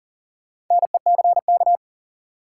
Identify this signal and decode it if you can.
Morse code, heard as a keyed tone, about 40 wpm, 700 Hz. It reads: NECK
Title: Sets of 5 Words Encore - New Word Drill - 40wpm